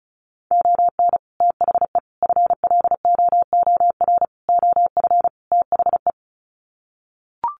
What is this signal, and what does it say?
Morse code, heard as a keyed tone, about 34 wpm, 700 Hz. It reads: ON THE FLOOR OF THE
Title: Sets of 5 Words 35wpm